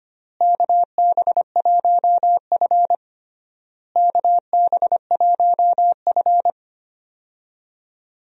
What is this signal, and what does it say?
Morse code, heard as a keyed tone, about 25 wpm, 700 Hz. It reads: KB1F KB1F
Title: USA Call Signs - 4 Characters - Repeated 3x - 25wpm